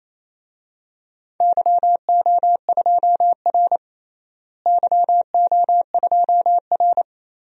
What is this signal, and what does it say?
Morse code, heard as a keyed tone, about 27 wpm, 700 Hz. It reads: YO2R YO2R
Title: DXCC-100 Call Signs - 4 Characters - Repeated 3x - 28wpm